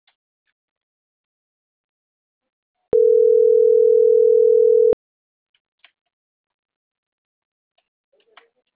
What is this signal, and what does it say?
Ringback tone (North America), 440 + 480 Hz.